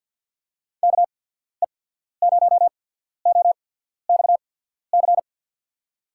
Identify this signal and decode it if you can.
Morse code, heard as a keyed tone, about 53 wpm, spaced as for 14 wpm, 700 Hz. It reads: KE0OXC
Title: USA Call Signs - 6 Characters - 8x Character Spacing - 50wpm